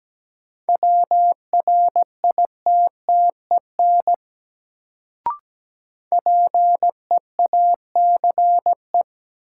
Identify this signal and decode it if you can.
Morse code, heard as a keyed tone, about 17 wpm, 700 Hz. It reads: WRITTEN PEACE
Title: Sentences from Top 1500 Words - New Word Drill Alt - 17wpm